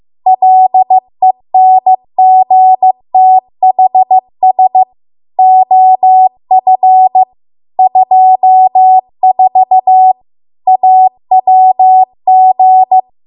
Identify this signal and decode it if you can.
Morse code, heard as a keyed tone, about 15 wpm, 750 Hz. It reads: LENGTHS OF 24 AWG